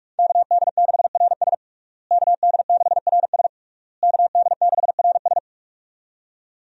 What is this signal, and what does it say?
Morse code, heard as a keyed tone, about 41 wpm, 700 Hz. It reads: KD6RS KD6RS KD6RS
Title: USA Call Signs - 5 Characters - Repeated 3x - 45wpm